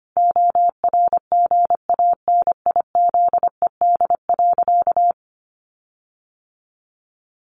Morse code, heard as a keyed tone, about 25 wpm, 700 Hz. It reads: ORGANIZED.
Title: Sentences from Top 1300 Words 25wpm